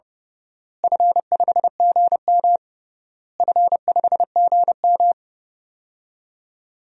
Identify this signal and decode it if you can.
Morse code, heard as a keyed tone, about 29 wpm, 700 Hz. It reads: F5GM F5GM
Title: DXCC-100 Call Signs - 4 Characters - Repeated 3x - 30wpm